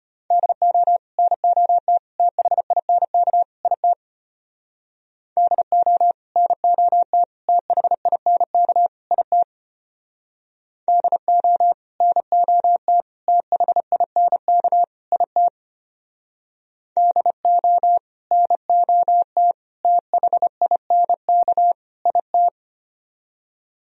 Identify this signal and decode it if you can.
Morse code, heard as a keyed tone, about 30 wpm, 700 Hz. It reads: DO NOT THINK IT DO NOT THINK IT DO NOT THINK IT DO NOT THINK IT
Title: Speed-Racing - Sets of 4 Words 25wpm